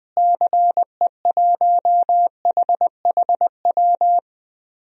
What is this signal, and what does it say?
Morse code, heard as a keyed tone, about 20 wpm, 700 Hz. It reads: CE1HHW
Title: DXCC-100 Call Signs - 6 Characters - Repeated 3x - 20wpm